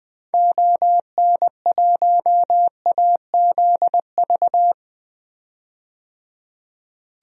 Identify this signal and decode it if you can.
Morse code, heard as a keyed tone, about 21 wpm, 700 Hz. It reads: ON1AZV
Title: DXCC-100 Call Signs - 6 Characters - 20wpm